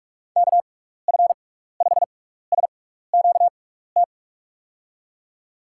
Morse code, heard as a keyed tone, about 46 wpm, spaced as for 15 wpm, 700 Hz. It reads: KF5SQT
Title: USA Call Signs - 6 Characters - 6x Character Spacing - 45wpm